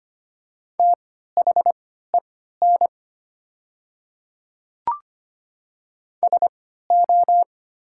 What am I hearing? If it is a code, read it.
Morse code, heard as a keyed tone, about 25 wpm, spaced as for 14 wpm, 700 Hz. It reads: THEN SO
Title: 3x Character Spacing - Top 100 Words - 25wpm